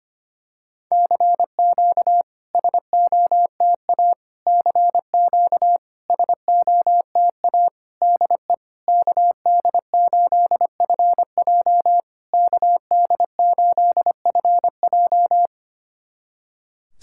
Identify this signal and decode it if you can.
Morse code, heard as a keyed tone, about 25 wpm, 700 Hz. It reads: CQ SOTA CQ SOTA DE KD8FJ KD8FJ
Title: SOTA QSO Practice 25wpm